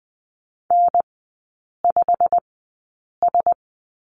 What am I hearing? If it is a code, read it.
Morse code, heard as a keyed tone, about 20 wpm, spaced as for 8 wpm, 700 Hz. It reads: N5S
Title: 152 Lesson 25e - Letter p - Three Characters 20wpm